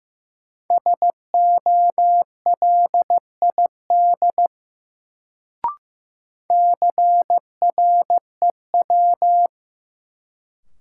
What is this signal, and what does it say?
Morse code, heard as a keyed tone, about 15 wpm, 700 Hz. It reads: SOLID CREW
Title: Sentences from Top 1500 Words - New Word Drill - 15wpm